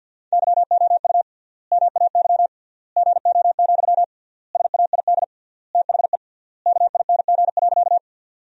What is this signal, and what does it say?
Morse code, heard as a keyed tone, about 53 wpm, 700 Hz. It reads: YOU MAY GO, SAID THE KING.
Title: 3x Word-Spacing - Sentences from Top 400 Words - 50wpm